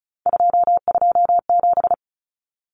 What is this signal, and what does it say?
Morse code, heard as a keyed tone, about 34 wpm, 700 Hz. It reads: 227
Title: Numbers - 3-Digit 35wpm